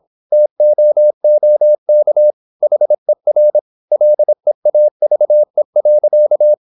Morse code, heard as a keyed tone, about 26 wpm, 600 Hz. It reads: TOOK HER LEAVE.
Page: Never 26wpm | CW med Gnister